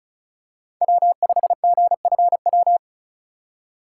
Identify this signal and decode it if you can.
Morse code, heard as a keyed tone, about 36 wpm, 700 Hz. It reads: W5GFW
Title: USA Call Signs - 5 Characters - 3x Character Spacing - 35wpm